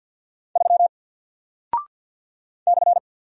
Morse code, heard as a keyed tone, about 48 wpm, 700 Hz. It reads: 3 /
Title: Single Character (Letter:Number:Punctuation:Prosign) Alt 50wpm